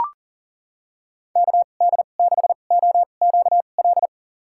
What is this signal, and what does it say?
Morse code, heard as a keyed tone, about 39 wpm, spaced as for 28 wpm, 700 Hz. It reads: KD6OQL
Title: USA Call Signs - 6 Characters - 2x Character Spacing - 40wpm